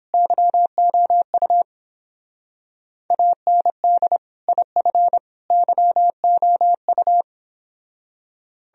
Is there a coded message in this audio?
Yes: Morse code from 0.1 to 7.2 s, heard as a keyed tone, about 29 wpm, 700 Hz. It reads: YOU AND IF YOU